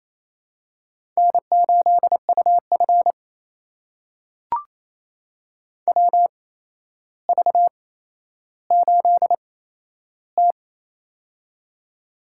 Morse code, heard as a keyed tone, about 28 wpm, spaced as for 12 wpm, 700 Hz. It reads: N8UF WV8T
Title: USA Call Signs - 4 Characters - 8x Character Spacing - 28wpm